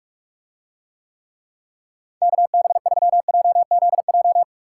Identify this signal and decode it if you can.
Morse code, heard as a keyed tone, about 49 wpm, 700 Hz. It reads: KB3JZJ